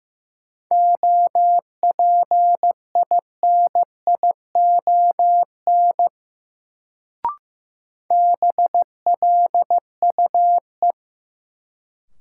Morse code, heard as a keyed tone, about 15 wpm, 700 Hz. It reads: OPINION BLUE
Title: Sentences from Top 1200 Words - New Word Drill Alt - 15wpm